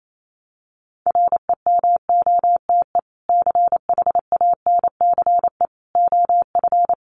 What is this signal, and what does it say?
Morse code, heard as a keyed tone, about 28 wpm, 700 Hz. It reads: REMOTE CHANCE OF
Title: Sentences from Top 1300 Words 28wpm